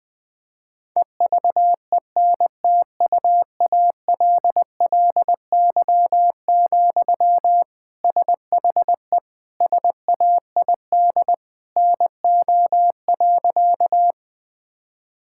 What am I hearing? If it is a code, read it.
Morse code, heard as a keyed tone, about 21 wpm, 700 Hz. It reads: EVENTUALLY, SHE SAID NO.
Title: Sentences from Top 1000 Words - Review Part 6 - 20wpm